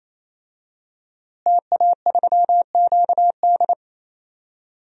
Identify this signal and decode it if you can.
Morse code, heard as a keyed tone, about 27 wpm, 700 Hz. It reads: TA3QD
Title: DXCC-100 Call Signs - 5 Characters - Repeated 3x - 28wpm